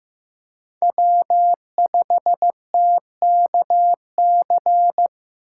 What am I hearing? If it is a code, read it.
Morse code, heard as a keyed tone, about 15 wpm, 700 Hz. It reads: W5TKC
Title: USA Call Signs - 5 Characters - 4x Character Spacing - 15wpm